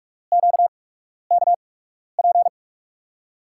Morse code, heard as a keyed tone, about 47 wpm, spaced as for 12 wpm, 700 Hz. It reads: QKP
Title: USA Call Signs - 5 Characters - 8x Character Spacing - 45wpm